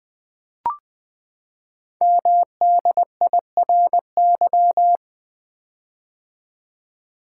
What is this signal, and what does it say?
Morse code, heard as a keyed tone, about 20 wpm, 700 Hz. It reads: MDIRY
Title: Code Groups - 5 Letters - 20wpm